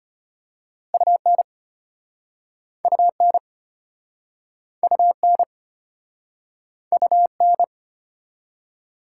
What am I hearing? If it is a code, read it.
Morse code, heard as a keyed tone, about 32 wpm, 700 Hz. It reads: UN UN UN UN